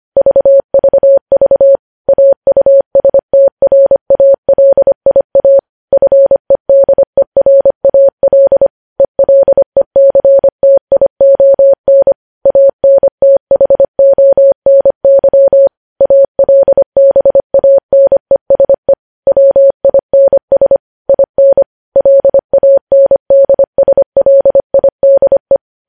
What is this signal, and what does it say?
Morse code, heard as a keyed tone, about 25 wpm, 550 Hz. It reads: VVV AUSTRALIA FEDERAL ELECTION ANTHONY ALBANESE WINS IN LANDSLIDE
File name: podcast / News headlines-1x-25x25